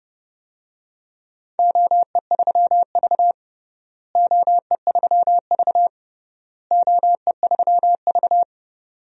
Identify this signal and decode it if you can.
Morse code, heard as a keyed tone, about 30 wpm, 700 Hz. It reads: OE3V OE3V OE3V
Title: DXCC-100 Call Signs - 4 Characters - Repeated 3x - 30wpm